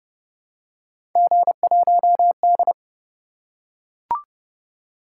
Morse code, heard as a keyed tone, about 28 wpm, 700 Hz. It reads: G1D